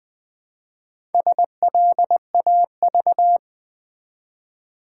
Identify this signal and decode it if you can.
Morse code, heard as a keyed tone, about 20 wpm, 700 Hz. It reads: SLAV
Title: Build A Word - 1 Letter At A Time - 3 to 5-Letter Words - 20wpm